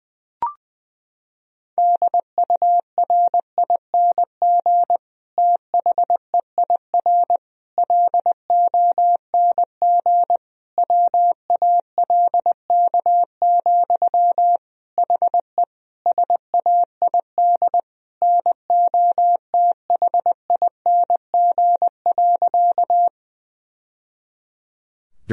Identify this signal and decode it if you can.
Morse code, heard as a keyed tone, about 20 wpm, 700 Hz. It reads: DURING THEIR LONG WALK, HE SAID NOTHING.
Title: Sentences from Top 1000 Words - Review Part 2 - 20wpm